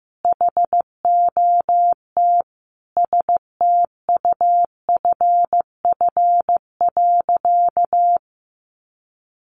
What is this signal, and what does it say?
Morse code, heard as a keyed tone, about 15 wpm, 700 Hz. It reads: HOT STUFF.
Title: Sentences from Top 1400 Words 15wpm